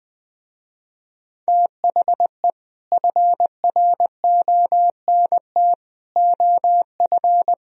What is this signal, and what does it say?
Morse code, heard as a keyed tone, about 20 wpm, 700 Hz. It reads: THE FRONT OF
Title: Speed-Racing - Sets of 4 Words 17wpm